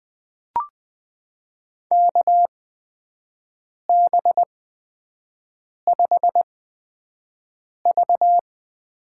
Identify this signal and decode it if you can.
Morse code, heard as a keyed tone, about 20 wpm, spaced as for 5 wpm, 700 Hz. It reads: KB5V
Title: USA Call Signs - 6 Characters - 8x Character Spacing - 20wpm